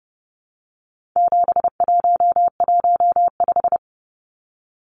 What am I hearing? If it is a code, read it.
Morse code, heard as a keyed tone, about 30 wpm, 700 Hz. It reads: 7115
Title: Numbers - 4-Digit 30wpm